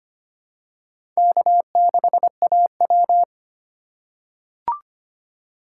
Morse code, heard as a keyed tone, about 25 wpm, 700 Hz. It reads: K6AW
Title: USA Call Signs - 4 Characters - 4x Character Spacing - 25wpm